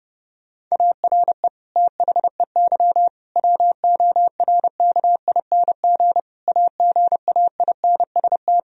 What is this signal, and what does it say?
Morse code, heard as a keyed tone, about 30 wpm, 700 Hz. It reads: ARE THEY WORKING AGAINST
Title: Sentences from Top 1000 Words - Review Part 4 - 30wpm